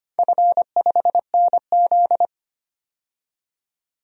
Morse code, heard as a keyed tone, about 25 wpm, 700 Hz. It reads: F5NZ
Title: DXCC-100 Call Signs - 4 Characters - 25wpm